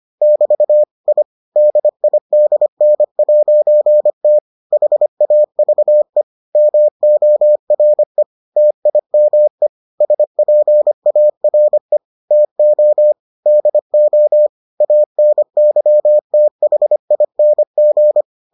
Morse code, heard as a keyed tone, about 25 wpm, 600 Hz. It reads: <BT> I DIDN'T HAVE MORE TIME SPARE TO DO ANYTHING